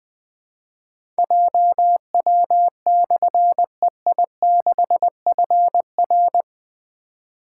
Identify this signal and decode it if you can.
Morse code, heard as a keyed tone, about 20 wpm, 700 Hz. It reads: JW/EI6FR
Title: Difficult Call Signs 20wpm